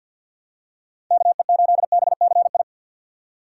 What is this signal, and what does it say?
Morse code, heard as a keyed tone, about 51 wpm, 700 Hz. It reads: KE8BCI
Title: USA Call Signs - 6 Characters - 3x Character Spacing - 50wpm